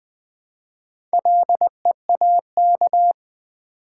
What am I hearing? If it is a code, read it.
Morse code, heard as a keyed tone, about 20 wpm, 700 Hz. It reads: LEAK
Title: Build A Word - 1 Letter At A Time - 4-Letter Words - 20wpm